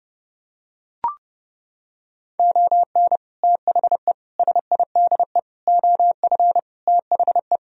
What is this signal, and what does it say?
Morse code, heard as a keyed tone, about 30 wpm, 700 Hz. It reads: ON THE SIDE OF THE